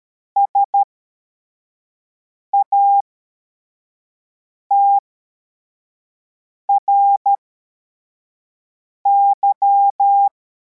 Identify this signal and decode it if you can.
Morse code, heard as a keyed tone, about 13 wpm, spaced as for 4 wpm, 800 Hz. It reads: SATRY